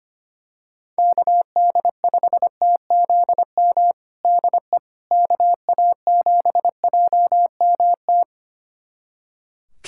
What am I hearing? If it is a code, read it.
Morse code, heard as a keyed tone, about 25 wpm, 700 Hz. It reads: KD5TZM DE KA7JMT
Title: USA Callsign DE Callsign 25wpm